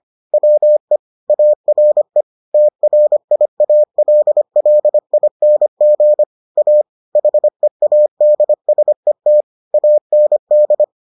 Morse code, heard as a keyed tone, about 25 wpm, 600 Hz. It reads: WE ARE TRIALLING A HEADSET AND